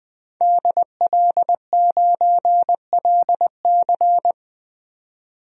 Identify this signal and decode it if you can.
Morse code, heard as a keyed tone, about 20 wpm, 700 Hz. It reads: DL9LC
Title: DXCC-100 Call Signs - 5 Characters - 20wpm